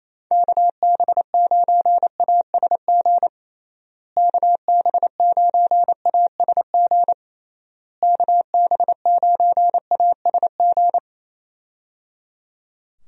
Morse code, heard as a keyed tone, about 28 wpm, 700 Hz. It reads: KB9ASG KB9ASG KB9ASG
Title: USA Call Signs - 6 Characters - Repeated 3x - 28wpm